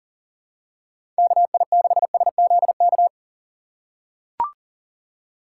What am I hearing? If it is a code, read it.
Morse code, heard as a keyed tone, about 40 wpm, 700 Hz. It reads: KI6SZK
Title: USA Call Signs - 6 Characters - 2x Character Spacing - 40wpm